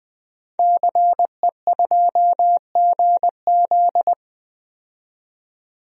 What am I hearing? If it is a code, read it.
Morse code, heard as a keyed tone, about 20 wpm, 700 Hz. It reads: CE2GZ
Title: DXCC-100 Call Signs - 5 Characters - 20wpm